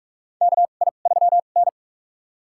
Morse code, heard as a keyed tone, about 44 wpm, spaced as for 32 wpm, 700 Hz. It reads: KI3N
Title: USA Call Signs - 4 Characters - 2x Character Spacing - 45wpm